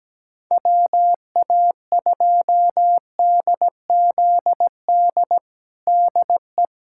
Morse code, heard as a keyed tone, about 17 wpm, 700 Hz. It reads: WA2DZD DE
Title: USA Callsign DE Callsign - 6 Characters - 17wpm